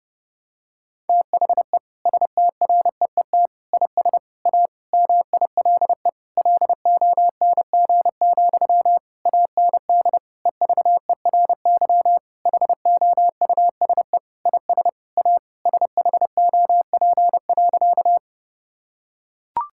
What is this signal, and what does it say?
Morse code, heard as a keyed tone, about 30 wpm, 700 Hz. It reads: THE STREET IS A MILE LONG, AND EVERY HOUSE IS A SHOP.
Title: Sentences from Top 1600 Words 30wpm